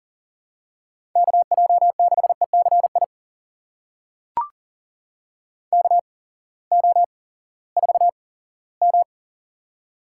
Morse code, heard as a keyed tone, about 38 wpm, spaced as for 20 wpm, 700 Hz. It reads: KJ6ECI KO4M
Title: USA Call Signs - 8x Character Spacing - 40wpm